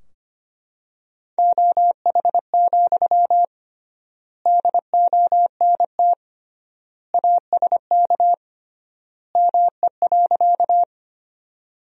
Morse code, heard as a keyed tone, about 25 wpm, 700 Hz. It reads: OH, DONT ASK ME.
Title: 3x Word-Spacing - Sentences from Top 500 Words - 25wpm